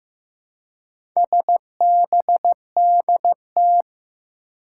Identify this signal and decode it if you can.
Morse code, heard as a keyed tone, about 15 wpm, 700 Hz. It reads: SBDT
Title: Four Challenging Letter-Number Combinations 15wpm